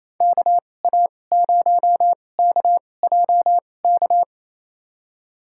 Morse code, heard as a keyed tone, about 28 wpm, spaced as for 20 wpm, 700 Hz. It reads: KA0KJK
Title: USA Call Signs - 6 Characters - 2x Character Spacing - 28wpm